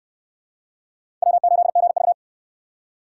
Morse code, heard as a keyed tone, about 61 wpm, 700 Hz. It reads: W8GV